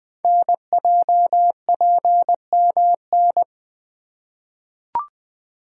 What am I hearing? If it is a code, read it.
Morse code, heard as a keyed tone, about 20 wpm, 700 Hz. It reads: NJPMN
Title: Code Groups - 5 Letters - 20wpm